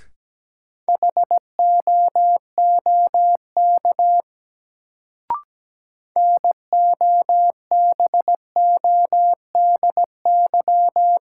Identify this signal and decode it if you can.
Morse code, heard as a keyed tone, about 17 wpm, 700 Hz. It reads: HOOK NOBODY
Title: Sentences from Top 1600 Words - New Word Drill Alt - 17wpm